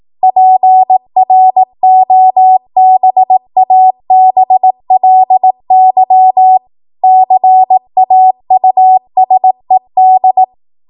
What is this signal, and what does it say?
Morse code, heard as a keyed tone, about 18 wpm, 750 Hz. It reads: PROBABLY CAUSED
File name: Morse / 18WPM